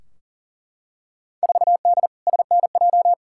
Morse code, heard as a keyed tone, about 40 wpm, 700 Hz. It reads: 4D SNJ
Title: Field Day Exchanges Encore - 40wpm